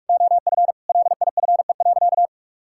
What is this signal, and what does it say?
Morse code, heard as a keyed tone, about 46 wpm, 700 Hz. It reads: OF LIFE.
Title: Sentences from Top 1000 Words - Review Part 5 - 45wpm